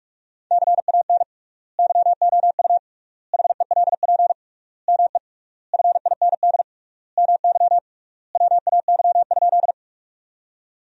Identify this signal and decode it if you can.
Morse code, heard as a keyed tone, about 43 wpm, 700 Hz. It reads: CAN YOU HELP ME FIND MY WAY?
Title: 3x Word-Spacing - Sentences from Top 200 Words - 45wpm